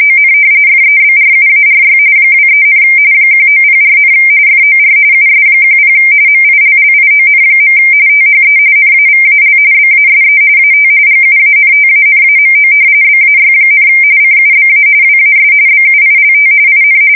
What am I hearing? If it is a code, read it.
RTTY, about 45.45 baud. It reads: ON FEBRUARY 23, 1904, FOR US10 MILLION. THE NAVAL BASE AT GUANTANAMO BAY, CUBA WAS BUILT IN
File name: fsk170 / usnavy3